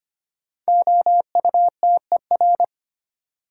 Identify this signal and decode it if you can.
Morse code, heard as a keyed tone, about 25 wpm, 700 Hz. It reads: OUTER
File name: 5 Letter Words - First 3 Letters Given - 25wpm